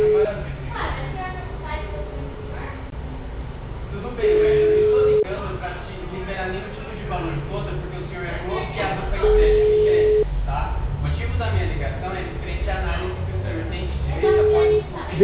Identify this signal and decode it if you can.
Ringback tone (Europe, 425 Hz), with speech.